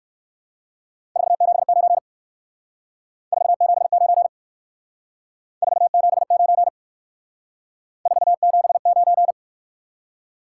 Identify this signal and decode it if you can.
Morse code, heard as a keyed tone, about 58 wpm, 700 Hz. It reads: 479 479 479 479